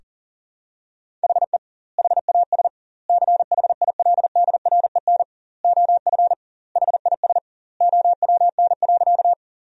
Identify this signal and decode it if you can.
Morse code, heard as a keyed tone, about 40 wpm, 700 Hz. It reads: HE HAS CHILDREN OF HIS OWN.
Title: 2x Word-Spacing - Sentences from Top 300 Words - 40wpm